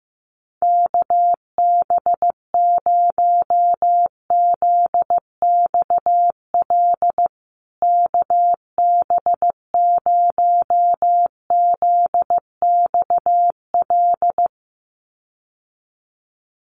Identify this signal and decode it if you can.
Morse code, heard as a keyed tone, about 15 wpm, 700 Hz. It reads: KB0ZXL KB0ZXL
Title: USA Call Signs Repeated 2x 15wpm